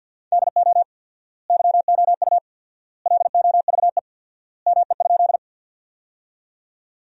Morse code, heard as a keyed tone, about 50 wpm, 700 Hz. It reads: DO YOU LOVE ME?
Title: 4x Word-Spacing - Sentences from Top 400 Words - 50wpm